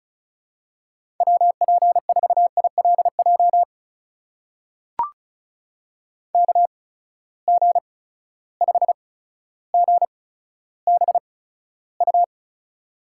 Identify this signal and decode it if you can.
Morse code, heard as a keyed tone, about 34 wpm, spaced as for 15 wpm, 700 Hz. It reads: WP4ILJ KG5GBU
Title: USA Call Signs - 6 Characters - 8x Character Spacing - 35wpm